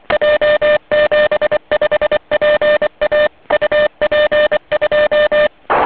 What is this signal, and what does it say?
Morse code, heard as a keyed tone, about 23 wpm, 590 Hz, with noise. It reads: J75PA UP2
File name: 160 cw